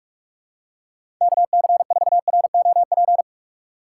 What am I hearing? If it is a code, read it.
Morse code, heard as a keyed tone, about 46 wpm, 700 Hz. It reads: KC4ROP
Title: USA Call Signs - 6 Characters - 2x Character Spacing - 45wpm